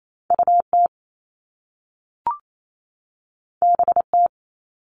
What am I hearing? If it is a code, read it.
Morse code, heard as a keyed tone, about 28 wpm, 700 Hz. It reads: UT BT
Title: All Cut Numbers - 2-Digit 28wpm